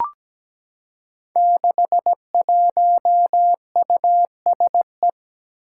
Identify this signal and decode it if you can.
Morse code, heard as a keyed tone, about 17 wpm, 700 Hz. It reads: 61USE
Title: Five Challenging Letter-Number Combinations 17wpm